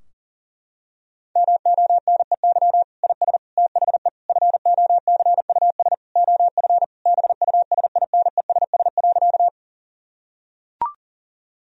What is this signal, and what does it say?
Morse code, heard as a keyed tone, about 40 wpm, 700 Hz. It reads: MONEY IS THE FOCUS OF BUSINESS.
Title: Sentences from Top 1000 Words - Review Part 4 - 40wpm